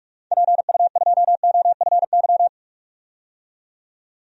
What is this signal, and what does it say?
Morse code, heard as a keyed tone, about 46 wpm, 700 Hz. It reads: PU2OFY
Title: DXCC-100 Call Signs - 6 Characters - 45wpm